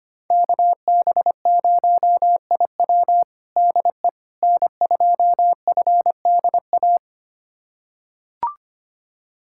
Morse code, heard as a keyed tone, about 25 wpm, 700 Hz. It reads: KB0IW DE N2FDA
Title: USA Callsign DE Callsign 25wpm